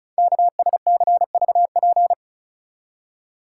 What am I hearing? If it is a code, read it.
Morse code, heard as a keyed tone, about 36 wpm, 690 Hz. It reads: KSCVP
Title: Code Groups - 5 Letters - 35wpm